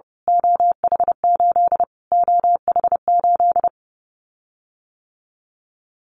Morse code, heard as a keyed tone, about 30 wpm, 690 Hz. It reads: OH8 OH8
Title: Callsign Prefixes for Top 25 DXCC - Repeated 3x - 30wpm